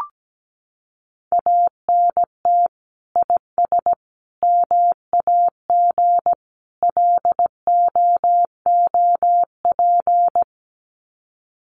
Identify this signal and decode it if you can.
Morse code, heard as a keyed tone, about 17 wpm, 700 Hz. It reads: ANT IS MAG LOOP
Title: Common QSO Phrases 17wpm